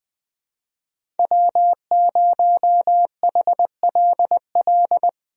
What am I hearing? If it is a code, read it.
Morse code, heard as a keyed tone, about 20 wpm, 700 Hz. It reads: W0HLL
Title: USA Call Signs - 5 Characters - Repeated 2x - 20wpm